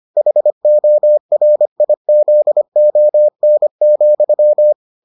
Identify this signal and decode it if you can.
Morse code, heard as a keyed tone, about 25 wpm, 600 Hz. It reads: HORIZON,